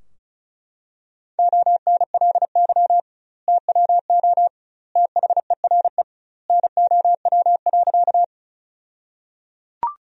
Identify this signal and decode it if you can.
Morse code, heard as a keyed tone, about 34 wpm, 700 Hz. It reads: ONLY TWO THERE NOW.
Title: 2x Word-Spacing - Sentences from Top 200 Words - 35wpm